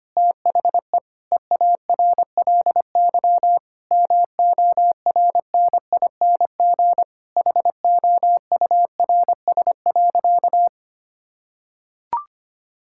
Morse code, heard as a keyed tone, about 25 wpm, 700 Hz. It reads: THE EARLY MORNING HOURS.
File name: practice / 25wpm / Difficult Sentences with Hints - 25wpm